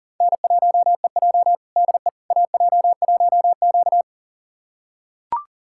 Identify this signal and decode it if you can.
Morse code, heard as a keyed tone, about 40 wpm, 700 Hz. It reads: N1EJ DE AJ1Q
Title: USA Callsign DE Callsign - 4 Characters - 40wpm